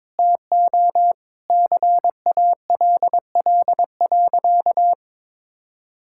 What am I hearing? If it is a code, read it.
Morse code, heard as a keyed tone, about 22 wpm, 700 Hz. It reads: TO CALL.
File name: practice / 22wpm / Sentences with 2-Word Phrases - 22wpm